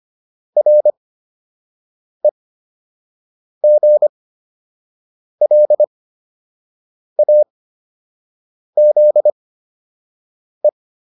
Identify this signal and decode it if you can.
Morse code, heard as a keyed tone, about 25 wpm, spaced as for 6 wpm, 600 Hz. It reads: REGLAZE
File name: RE7 06